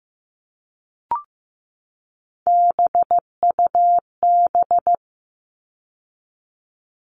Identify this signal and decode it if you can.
Morse code, heard as a keyed tone, about 15 wpm, 700 Hz. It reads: BUB
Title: All Cut Numbers - 3-Digit 15wpm